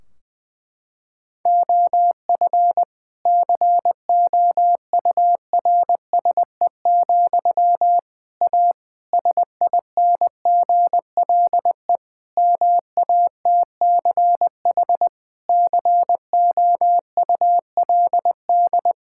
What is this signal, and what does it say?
Morse code, heard as a keyed tone, about 20 wpm, 700 Hz. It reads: OF COURSE, A SINGLE MATCH COULD
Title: Sentences from Top 1600 Words 20wpm